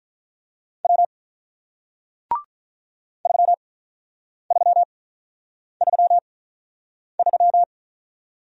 Morse code, heard as a keyed tone, about 47 wpm, spaced as for 9 wpm, 700 Hz. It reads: W 3333